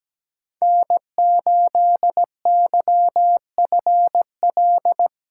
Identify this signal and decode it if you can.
Morse code, heard as a keyed tone, about 17 wpm, 700 Hz. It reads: N8YFL
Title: USA Call Signs - 5 Characters - 3x Character Spacing - 17wpm